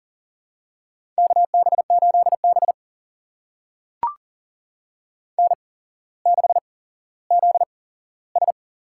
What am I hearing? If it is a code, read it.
Morse code, heard as a keyed tone, about 40 wpm, spaced as for 17 wpm, 700 Hz. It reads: KB8B N6ZS
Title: USA Call Signs - 4 Characters - 8x Character Spacing - 40wpm